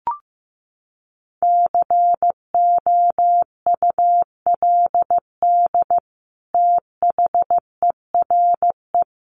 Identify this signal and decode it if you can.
Morse code, heard as a keyed tone, about 15 wpm, 700 Hz. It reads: COULD THERE
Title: Sentences from Top 1500 Words 15wpm